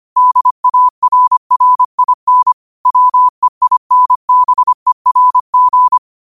Morse code, heard as a keyed tone, about 25 wpm, 1000 Hz. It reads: DARRIN WEINBERG